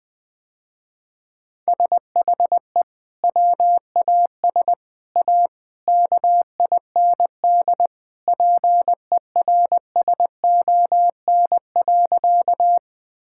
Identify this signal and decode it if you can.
Morse code, heard as a keyed tone, about 19 wpm, 700 Hz. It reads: SHE WAS A KIND PERSON.
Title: Sentences with 3-Word Phrases - Spoken Beginning - 20wpm